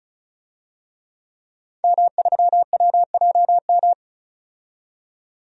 Morse code, heard as a keyed tone, about 36 wpm, 700 Hz. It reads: M3WJM
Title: DXCC-100 Call Signs - 5 Characters - 35wpm